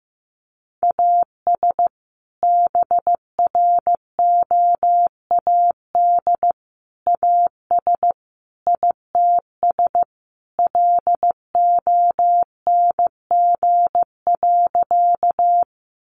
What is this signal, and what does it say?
Morse code, heard as a keyed tone, about 15 wpm, 700 Hz. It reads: AS BROAD AS ITS LONG.
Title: Sentences from Top 1000 Words 15wpm